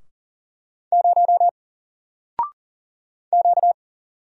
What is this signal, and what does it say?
Morse code, heard as a keyed tone, about 40 wpm, 700 Hz. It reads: 0 Q
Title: Single Letter-Number 40wpm